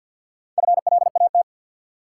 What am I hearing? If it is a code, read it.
Morse code, heard as a keyed tone, about 51 wpm, 700 Hz. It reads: FLAT
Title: Sentences from Top 1000 Words - New Word Drill - 50wpm